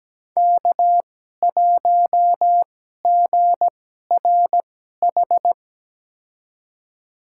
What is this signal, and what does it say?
Morse code, heard as a keyed tone, about 17 wpm, spaced as for 12 wpm, 700 Hz. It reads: K1GRH
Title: USA Call Signs - 5 Characters - 2x Character Spacing - 17wpm